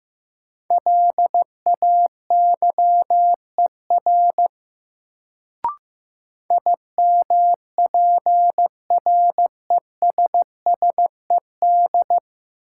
Morse code, heard as a keyed tone, about 15 wpm, 700 Hz. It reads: LAYER IMPRESSED
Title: Sentences from Top 1600 Words - New Word Drill - 15wpm